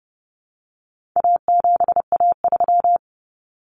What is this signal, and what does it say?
Morse code, heard as a keyed tone, about 30 wpm, 700 Hz. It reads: A7A3
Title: Common Cut Numbers - 4-Digit 30wpm